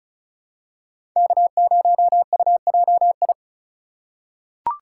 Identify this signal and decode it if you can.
Morse code, heard as a keyed tone, about 34 wpm, 690 Hz. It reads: K0UJI
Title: USA Call Signs - 5 Characters - 2x Character Spacing - 35wpm